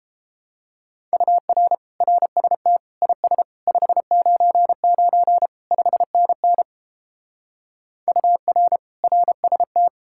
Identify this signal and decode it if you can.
Morse code, heard as a keyed tone, about 33 wpm, 700 Hz. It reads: UR RST IS 599 5NN UR RST